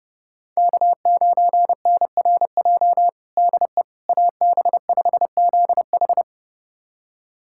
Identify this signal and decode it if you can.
Morse code, heard as a keyed tone, about 30 wpm, 700 Hz. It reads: K9NRJ DE AB5ZH
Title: USA Callsign DE Callsign - 5 Characters - 30wpm